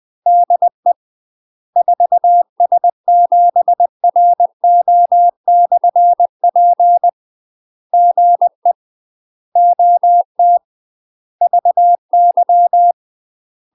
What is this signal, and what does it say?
Morse code, heard as a keyed tone, about 20 wpm, 700 Hz. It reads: DE 4S7RO/P GE OT VY
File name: UK MASTER QSOs TRAINING/20WPM